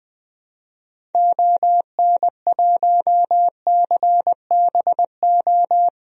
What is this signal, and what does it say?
Morse code, heard as a keyed tone, about 21 wpm, 700 Hz. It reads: ON1CBO
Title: DXCC-100 Call Signs - 6 Characters - 20wpm